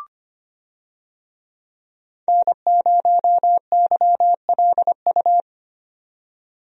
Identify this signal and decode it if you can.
Morse code, heard as a keyed tone, about 25 wpm, 700 Hz. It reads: N0YLU
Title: USA Call Signs - 5 Characters - 25wpm